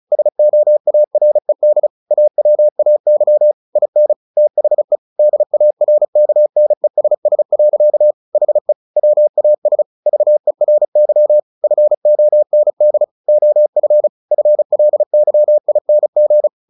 Morse code, heard as a keyed tone, about 34 wpm, 600 Hz. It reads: SOARED AWAY IN THE DARKNESS. HE WAS VERY FOND OF FLYING
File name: the old man and the sea / the old man and the sea 1 35wpm